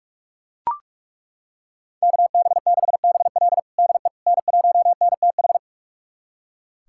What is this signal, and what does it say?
Morse code, heard as a keyed tone, about 42 wpm, 700 Hz. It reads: KB6BL DE N1NTH
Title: USA Callsign DE Callsign - 5 Characters - 45wpm